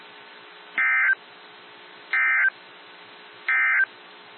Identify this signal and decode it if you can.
Emergency Alert System (SAME) tones, heard as harsh buzzing bursts. It is heard from 0.7 to 3.9 s.
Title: NOAA Weather Radio - WXJ73